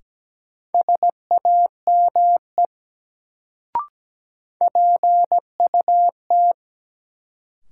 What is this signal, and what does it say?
Morse code, heard as a keyed tone, about 17 wpm, 700 Hz. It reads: SAME PUT
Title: Sentences from Top 200 Words - New Word Drill - 17wpm